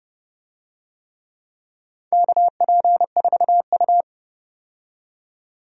Morse code, heard as a keyed tone, about 30 wpm, 700 Hz. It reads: KP4U
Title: DXCC-100 Call Signs - 4 Characters - 30wpm